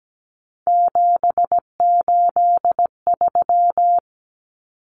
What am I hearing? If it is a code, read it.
Morse code, heard as a keyed tone, about 17 wpm, 700 Hz. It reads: 783
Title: Common Cut Numbers - 3-Digit 17wpm